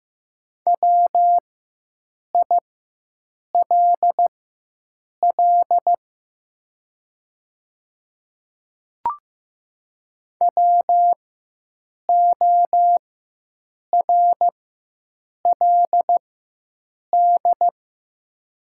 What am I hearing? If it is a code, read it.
Morse code, heard as a keyed tone, about 15 wpm, spaced as for 7 wpm, 700 Hz. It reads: WILL WORLD
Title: 4x Character Spacing - Top 300 Words - 15wpm